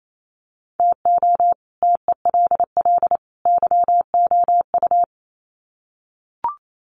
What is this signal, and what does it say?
Morse code, heard as a keyed tone, about 28 wpm, 700 Hz. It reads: TO TELL YOU
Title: Sets of 3 Words 28wpm